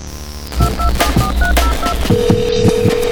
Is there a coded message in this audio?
No.